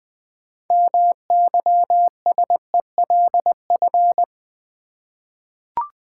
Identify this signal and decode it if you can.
Morse code, heard as a keyed tone, about 20 wpm, 700 Hz. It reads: MYSELF
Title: 6 Letter Words - First 3 Letters Given - 20wpm